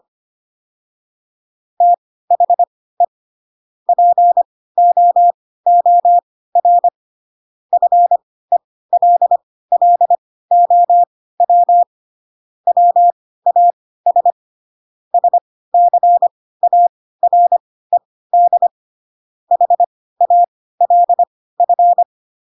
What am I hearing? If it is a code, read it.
Morse code, heard as a keyed tone, about 25 wpm, spaced as for 16 wpm, 700 Hz. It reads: THE POOR FELLOW WAS SCARED HALF
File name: Pinocchio / Pinocchio ch1 10 25 ik1hge